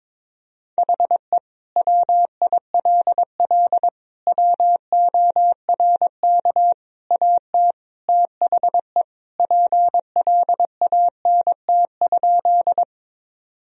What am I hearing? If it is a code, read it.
Morse code, heard as a keyed tone, about 22 wpm, 700 Hz. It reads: HE WILL WORK AT THE PLANT?
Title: Sentences from Top 1000 Words - Review Part 1 - 22wpm